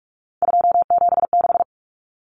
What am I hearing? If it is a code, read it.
Morse code, heard as a keyed tone, about 44 wpm, 700 Hz. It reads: 276
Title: Common Cut Numbers - 3-Digit 45wpm